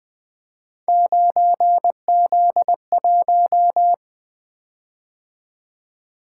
Morse code, heard as a keyed tone, about 20 wpm, 700 Hz. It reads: 9Z1
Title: Three Letter-Number Combinations - 20wpm